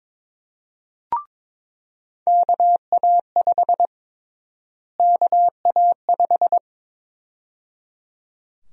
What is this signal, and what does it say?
Morse code, heard as a keyed tone, about 22 wpm, 700 Hz. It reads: KA5 KA5
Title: USA Call Sign Prefixes - Repeated 2x - 22wpm